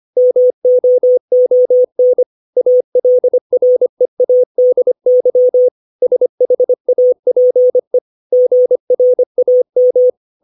Morse code, heard as a keyed tone, about 25 wpm, 500 Hz. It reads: MOON ALREADY SHAPE GRAM
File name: 632 words / 632 Words 25WPM